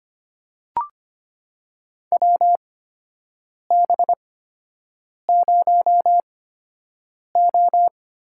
Morse code, heard as a keyed tone, about 25 wpm, spaced as for 7 wpm, 700 Hz. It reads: WB0O
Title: USA Call Signs - 8x Character Spacing - 25wpm